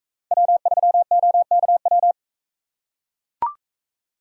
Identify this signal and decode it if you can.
Morse code, heard as a keyed tone, about 41 wpm, 700 Hz. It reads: W3OKW